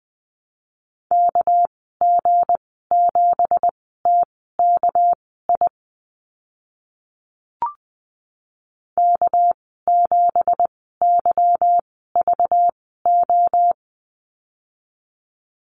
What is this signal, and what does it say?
Morse code, heard as a keyed tone, about 20 wpm, spaced as for 15 wpm, 700 Hz. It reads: KG7TKI K7YVO
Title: 219 Lesson 35g - Number 0 - Call Signs 20wpm